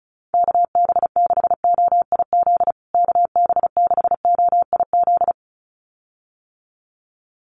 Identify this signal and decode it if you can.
Morse code, heard as a keyed tone, about 34 wpm, 700 Hz. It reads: KB6OIZ KB6OIZ
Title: USA Call Signs Repeated 2x 35wpm